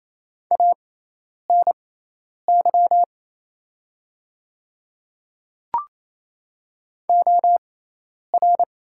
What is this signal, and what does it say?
Morse code, heard as a keyed tone, about 28 wpm, spaced as for 10 wpm, 700 Hz. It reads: ANY OR
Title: 6x Character Spacing - Top 200 Words - 28wpm